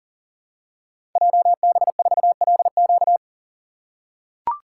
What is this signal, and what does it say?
Morse code, heard as a keyed tone, about 39 wpm, 700 Hz. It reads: JB4LQ
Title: Code Groups - 5 Letter:Numbers - 40wpm